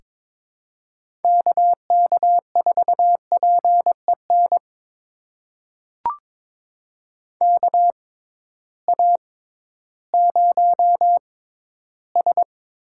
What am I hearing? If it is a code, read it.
Morse code, heard as a keyed tone, about 22 wpm, spaced as for 13 wpm, 700 Hz. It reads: KK4PEN KA0S
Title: USA Call Signs - 6 Characters - 6x Character Spacing - 22wpm